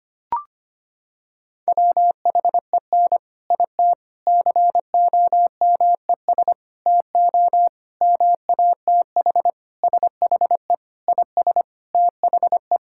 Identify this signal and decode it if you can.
Morse code, heard as a keyed tone, about 25 wpm, 700 Hz. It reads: WHEN IT COMES TO MATH SHE IS THE
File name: practice / 25wpm / Sentences with 4-Word Phrases - 25wpm